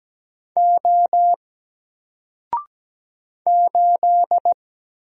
Morse code, heard as a keyed tone, about 17 wpm, 700 Hz. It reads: O 8
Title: Single Character (Letter:Number:Punctuation:Prosign) 17wpm